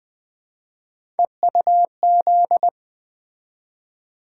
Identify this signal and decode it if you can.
Morse code, heard as a keyed tone, about 20 wpm, 700 Hz. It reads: EUZ
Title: Three Letter-Number Combinations - 20wpm